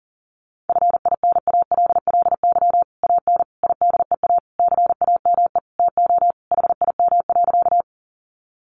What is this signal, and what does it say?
Morse code, heard as a keyed tone, about 40 wpm, 700 Hz. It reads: FINALLY AN IDEA CAME TO HIM.
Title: Sentences from Top 1000 Words 40wpm